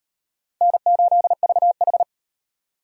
Morse code, heard as a keyed tone, about 33 wpm, 700 Hz. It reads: N8VH